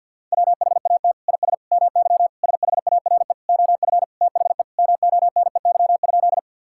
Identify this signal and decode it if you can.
Morse code, heard as a keyed tone, about 53 wpm, 700 Hz. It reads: WHAT IS MY SHARE OF THE MONEY?
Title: Sentences from Top 1000 Words - Review Part 6 - 50wpm